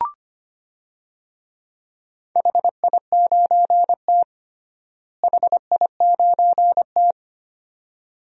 Morse code, heard as a keyed tone, about 25 wpm, 700 Hz. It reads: HI9T HI9T
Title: DXCC-100 Call Signs - 4 Characters - Repeated 2x - 25wpm